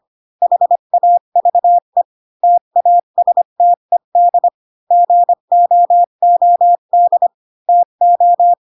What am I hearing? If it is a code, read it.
Morse code, heard as a keyed tone, about 25 wpm, 700 Hz. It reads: HAVE TASTED GOOD TO
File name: Pinocchio / Pinocchio ch5 20 25 ik1hge